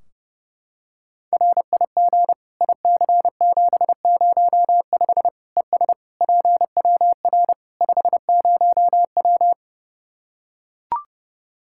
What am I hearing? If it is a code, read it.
Morse code, heard as a keyed tone, about 32 wpm, 700 Hz. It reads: RIG IC705 ES PWR 50W
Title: Practice QSO - Exchange 2 - Info - 30wpm